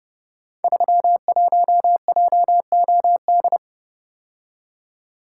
Morse code, heard as a keyed tone, about 30 wpm, 700 Hz. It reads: 31JOD
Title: Code Groups - 5 Letter:Numbers - 30wpm